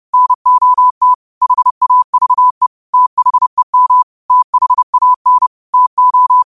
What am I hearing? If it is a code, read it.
Morse code, heard as a keyed tone, about 30 wpm, 1000 Hz. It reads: NOT HAVE THEM THAN TO
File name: 30wpm